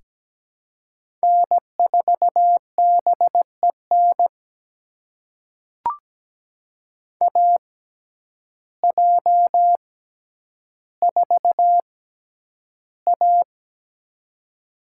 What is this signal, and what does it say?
Morse code, heard as a keyed tone, about 17 wpm, spaced as for 9 wpm, 700 Hz. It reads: N4BEN AJ4A
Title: USA Call Signs - 6x Character Spacing - 17wpm